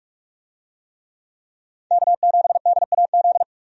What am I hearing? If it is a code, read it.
Morse code, heard as a keyed tone, about 43 wpm, 700 Hz. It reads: K7DAZ
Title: USA Call Signs - 5 Characters - 45wpm